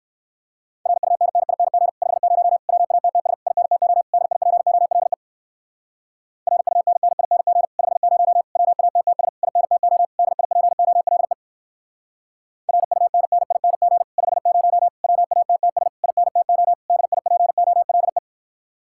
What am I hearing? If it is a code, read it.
Morse code, heard as a keyed tone, about 57 wpm, 700 Hz. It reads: RUNNING 50 WATTS INTO DIPOLE RUNNING 50 WATTS INTO DIPOLE RUNNING 50 WATTS INTO DIPOLE